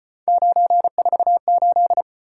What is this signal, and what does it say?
Morse code, heard as a keyed tone, about 33 wpm, 700 Hz. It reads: 948